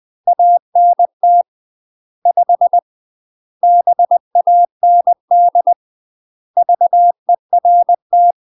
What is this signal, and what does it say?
Morse code, heard as a keyed tone, about 20 wpm, 700 Hz. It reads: ANT 5 BAND VERT
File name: UK MASTER QSOs TRAINING / 20WPM / UK MASTER QSOs 0053